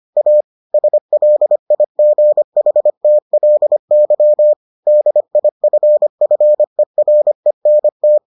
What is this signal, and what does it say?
Morse code, heard as a keyed tone, about 25 wpm, 600 Hz. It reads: A SLIGHTLY DIFFERENT